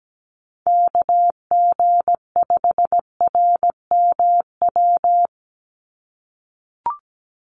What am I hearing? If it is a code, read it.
Morse code, heard as a keyed tone, about 17 wpm, 700 Hz. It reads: KG5RMW
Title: USA Call Signs 17wpm